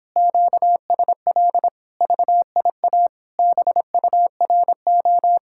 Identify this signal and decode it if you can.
Morse code, heard as a keyed tone, about 26 wpm, 700 Hz. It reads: QSL VIA BURO